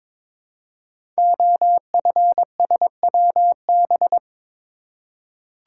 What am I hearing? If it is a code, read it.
Morse code, heard as a keyed tone, about 22 wpm, 700 Hz. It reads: OFSWB
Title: Code Groups - 5 Letters - 22wpm